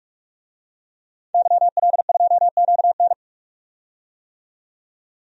Morse code, heard as a keyed tone, about 43 wpm, 700 Hz. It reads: YL2QN
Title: DXCC-100 Call Signs - 5 Characters - 45wpm